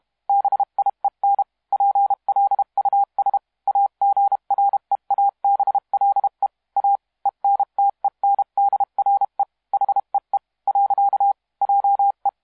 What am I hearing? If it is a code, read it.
Morse code, heard as a keyed tone, about 32 wpm, 800 Hz. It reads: BIEN PLUS AGREABLE A ENTENDRE HEE . JE
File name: musiqueradio / cw hsc demo f6ddr